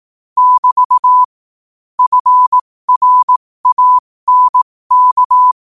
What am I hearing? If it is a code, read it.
Morse code, heard as a keyed tone, about 18 wpm, spaced as for 15 wpm, 1000 Hz. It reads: <BT> FRANK